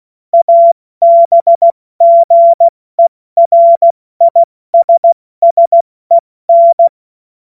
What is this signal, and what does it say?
Morse code, heard as a keyed tone, about 16 wpm, 680 Hz. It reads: ABGERISSEN
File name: Morsekurs / Witze / Witz 17. Mai 2022 70 F80